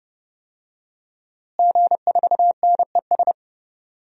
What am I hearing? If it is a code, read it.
Morse code, heard as a keyed tone, about 31 wpm, 700 Hz. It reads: G4NES
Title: DXCC-100 Call Signs - 5 Characters - Repeated 2x - 30wpm